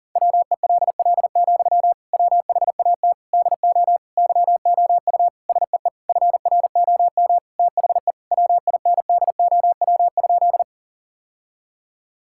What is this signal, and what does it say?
Morse code, heard as a keyed tone, about 40 wpm, 700 Hz. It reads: WELL, WHAT DO YOU SEE FROM THE WINDOW?
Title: Sentences from Top 1600 Words 40wpm